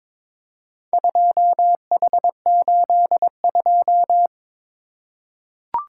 Morse code, heard as a keyed tone, about 22 wpm, 700 Hz. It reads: 2H82
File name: practice / 22wpm / Four Letter-Number Combinations - 22wpm